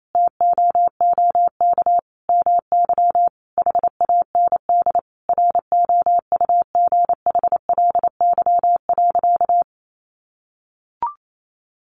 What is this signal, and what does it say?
Morse code, heard as a keyed tone, about 28 wpm, 700 Hz. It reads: TOOK MY HAND ROUGHLY.
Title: Sentences from Top 1500 Words 28wpm